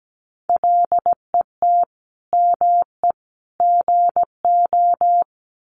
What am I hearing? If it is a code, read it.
Morse code, heard as a keyed tone, about 17 wpm, 700 Hz. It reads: LET ME GO
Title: Sentences from Top 300 Words 17wpm